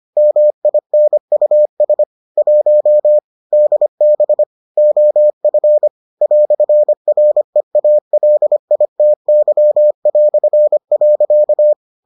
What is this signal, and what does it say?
Morse code, heard as a keyed tone, about 25 wpm, 600 Hz. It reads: MINUS 1 DB OF "REALITY".